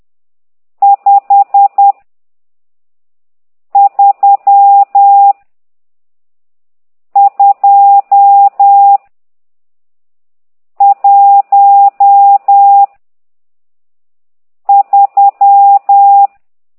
Morse code, heard as a keyed tone, about 10 wpm, 800 Hz. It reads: 5 3 2 1 3
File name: ct2idw / exerc12